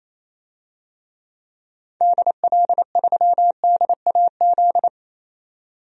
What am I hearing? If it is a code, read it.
Morse code, heard as a keyed tone, about 28 wpm, 700 Hz. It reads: DL3DAZ
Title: DXCC-100 Call Signs - 6 Characters - 28wpm